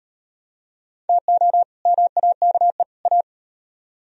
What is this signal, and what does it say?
Morse code, heard as a keyed tone, about 36 wpm, 700 Hz. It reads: TO MAKE A